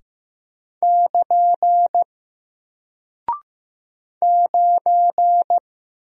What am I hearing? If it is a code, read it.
Morse code, heard as a keyed tone, about 15 wpm, 700 Hz. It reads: <KN> 9
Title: Single Character (Letter:Number:Punctuation:Prosign) Alt 15wpm